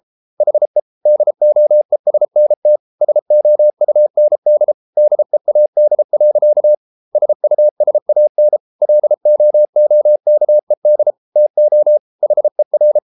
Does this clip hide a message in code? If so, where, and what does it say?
Yes: Morse code from 0.4 to 13.0 s, heard as a keyed tone, about 33 wpm, 600 Hz. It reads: HE DOESNT SOUND DEAD. SUSAN LOOKED TO HER